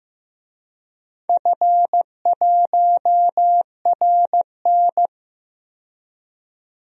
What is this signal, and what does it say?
Morse code, heard as a keyed tone, about 15 wpm, 700 Hz. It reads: F1RN
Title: DXCC-100 Call Signs - 4 Characters - 15wpm